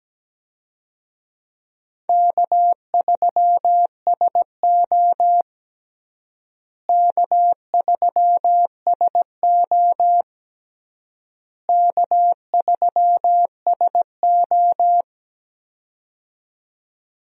Morse code, heard as a keyed tone, about 17 wpm, 700 Hz. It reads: K3SO K3SO K3SO
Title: USA Call Signs - 4 Characters - Repeated 3x - 17wpm